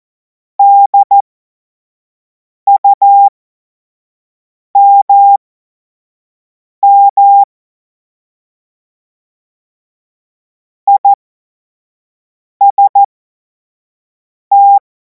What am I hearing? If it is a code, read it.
Morse code, heard as a keyed tone, about 14 wpm, spaced as for 5 wpm, 800 Hz. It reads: DUMM IST
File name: Morsekurs / Witze / Witz 09. Juni 2026 25 F70